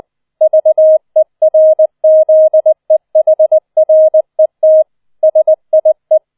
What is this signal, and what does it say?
Morse code, heard as a keyed tone, about 20 wpm, 630 Hz. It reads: VERZEHRET SIE